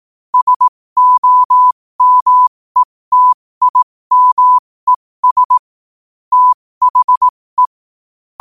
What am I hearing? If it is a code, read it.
Morse code, heard as a keyed tone, about 18 wpm, spaced as for 15 wpm, 1000 Hz. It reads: SOMETIMES THE
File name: qotdfiles / 15wpm